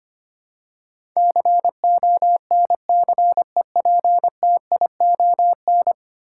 Morse code, heard as a keyed tone, about 25 wpm, 690 Hz. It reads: CONCEPTION
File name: 10 Letter Words - First 3 Letters Given - 25wpm